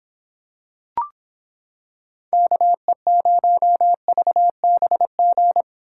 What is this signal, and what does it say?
Morse code, heard as a keyed tone, about 25 wpm, 700 Hz. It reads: KE0VBG